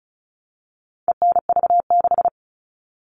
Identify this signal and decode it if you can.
Morse code, heard as a keyed tone, about 34 wpm, 700 Hz. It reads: ENV6
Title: All Cut Numbers - 4-Digit 35wpm